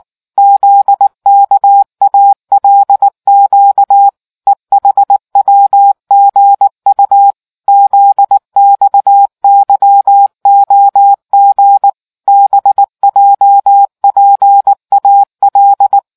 Morse code, heard as a keyed tone, about 19 wpm, 800 Hz. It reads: ZKALQ EHWGU ZXYOG BJPAL